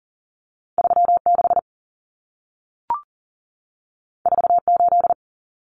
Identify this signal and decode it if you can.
Morse code, heard as a keyed tone, about 40 wpm, 700 Hz. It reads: 3 6 4 8
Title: Common Cut Numbers - 2-Digit 40wpm